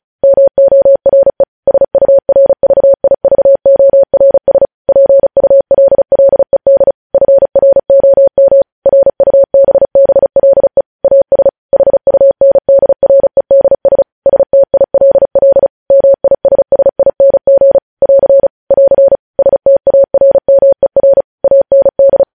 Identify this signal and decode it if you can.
Morse code, heard as a keyed tone, about 36 wpm, 550 Hz. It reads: MORE SURVIVORS PULLED FROM RUBBLE AS HUNDREDS STILL MISSING <AR> <AR> STARMER AND
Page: News headlines in Morse code at 35 wpm